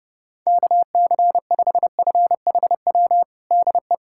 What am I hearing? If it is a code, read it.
Morse code, heard as a keyed tone, about 32 wpm, 700 Hz. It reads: KC5FHW DE
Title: USA Callsign DE Callsign 30wpm